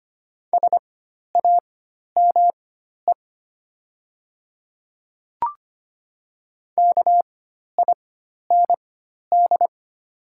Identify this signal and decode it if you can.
Morse code, heard as a keyed tone, about 25 wpm, spaced as for 12 wpm, 700 Hz. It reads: SAME KIND
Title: 4x Character Spacing - Top 300 Words - 25wpm